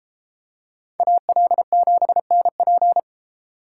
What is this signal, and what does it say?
Morse code, heard as a keyed tone, about 33 wpm, 700 Hz. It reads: AL7NP